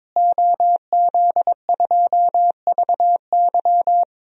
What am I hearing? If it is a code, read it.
Morse code, heard as a keyed tone, about 22 wpm, 700 Hz. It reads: OZ2VY
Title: DXCC-100 Call Signs - 5 Characters - Repeated 3x - 22wpm